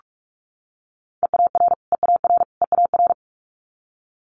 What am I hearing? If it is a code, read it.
Morse code, heard as a keyed tone, about 46 wpm, 700 Hz. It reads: EAR EAR EAR
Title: Multi-letter Phonemes Repeated 3x 45wpm